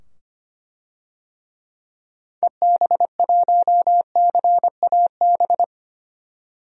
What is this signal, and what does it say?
Morse code, heard as a keyed tone, about 25 wpm, 700 Hz. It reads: EB1CAB
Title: DXCC-100 Call Signs - 6 Characters - 25wpm